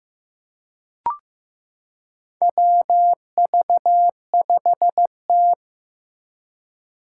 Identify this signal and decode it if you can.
Morse code, heard as a keyed tone, about 15 wpm, 700 Hz. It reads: WV5T
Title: Four Challenging Letter-Number Combinations 15wpm